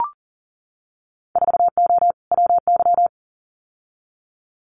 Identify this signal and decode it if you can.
Morse code, heard as a keyed tone, about 40 wpm, 700 Hz. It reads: 4O WY
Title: Winter Field Day Exchanges Encore 40wpm